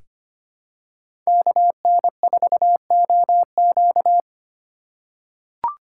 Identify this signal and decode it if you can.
Morse code, heard as a keyed tone, about 25 wpm, 700 Hz. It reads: KN4OQ
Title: USA Call Signs - 5 Characters - 3x Character Spacing - 25wpm